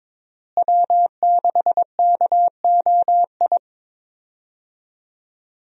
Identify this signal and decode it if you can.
Morse code, heard as a keyed tone, about 21 wpm, 700 Hz. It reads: W6KOI
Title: USA Call Signs - 5 Characters - Repeated 3x - 22wpm